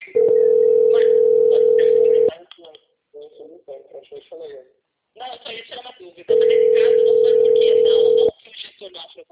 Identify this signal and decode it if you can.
Ringback tone (North America), 440 + 480 Hz, with speech.
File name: out- - - - - .12892